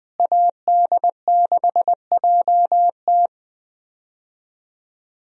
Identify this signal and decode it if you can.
Morse code, heard as a keyed tone, about 20 wpm, 700 Hz. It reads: AD6JT
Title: USA Call Signs - 5 Characters - Repeated 2x - 20wpm